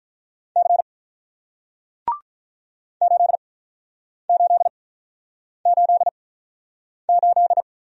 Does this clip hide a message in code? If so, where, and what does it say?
Yes: Morse code from 0.6 to 7.6 s, heard as a keyed tone, about 48 wpm, spaced as for 9 wpm, 700 Hz. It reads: C 8888